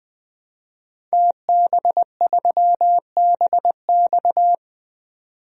Morse code, heard as a keyed tone, about 20 wpm, 700 Hz. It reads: TB3BX
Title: DXCC-100 Call Signs - 5 Characters - 20wpm